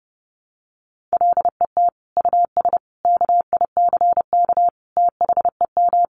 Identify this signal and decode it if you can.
Morse code, heard as a keyed tone, about 28 wpm, 700 Hz. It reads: LET US KICK THEM
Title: Sentences from Top 1300 Words 30wpm